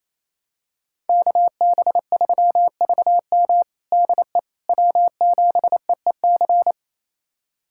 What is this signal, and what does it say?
Morse code, heard as a keyed tone, about 28 wpm, 700 Hz. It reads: KB3VM DE W7EEC
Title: USA Callsign DE Callsign - 5 Characters - 28wpm